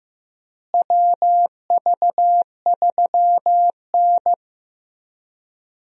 Morse code, heard as a keyed tone, about 15 wpm, 700 Hz. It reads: WV3N
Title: USA Call Signs - 4 Characters - Repeated 3x - 15wpm